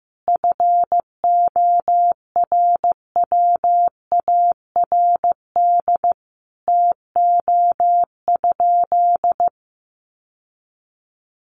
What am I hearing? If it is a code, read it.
Morse code, heard as a keyed tone, about 15 wpm, 700 Hz. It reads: FORWARD TO?
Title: Sentences from Top 1300 Words 15wpm